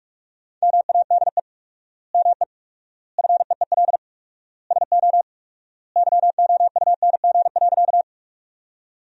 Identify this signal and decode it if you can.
Morse code, heard as a keyed tone, about 46 wpm, 700 Hz. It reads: MADE ME FEEL SO YOUNG.
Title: 4x Word-Spacing - Sentences from Top 400 Words - 45wpm